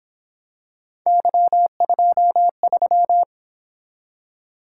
Morse code, heard as a keyed tone, about 27 wpm, 700 Hz. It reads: Y23
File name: Speed-Racing - Three Letter-Number Combinations - 22wpm